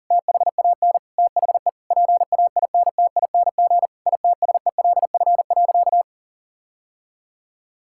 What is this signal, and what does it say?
Morse code, heard as a keyed tone, about 40 wpm, 700 Hz. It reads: THAN THE PAINTING ITSELF.
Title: Sentences from Top 1000 Words - Review Part 6 - 40wpm